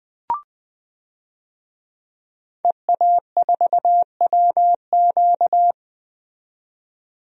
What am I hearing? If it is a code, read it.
Morse code, heard as a keyed tone, about 20 wpm, 700 Hz. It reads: EA4WQ
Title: DXCC-100 Call Signs - 5 Characters - 20wpm